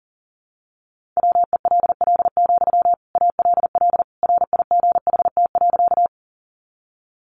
Morse code, heard as a keyed tone, about 40 wpm, 700 Hz. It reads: WELL, ALL RIGHT.
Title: Sentences from Top 200 Words 40wpm